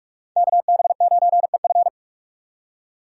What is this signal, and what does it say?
Morse code, heard as a keyed tone, about 44 wpm, 690 Hz. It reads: KB9EF
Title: USA Call Signs - 5 Characters - 6x Character Spacing - 45wpm